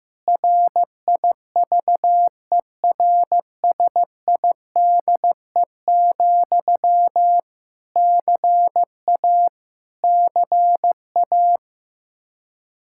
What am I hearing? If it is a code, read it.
Morse code, heard as a keyed tone, about 15 wpm, 700 Hz. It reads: RIVERSIDE, CA CA
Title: K1USN SST QSO Practice 15wpm